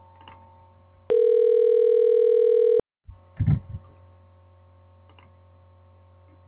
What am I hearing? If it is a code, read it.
Ringback tone (North America), 440 + 480 Hz.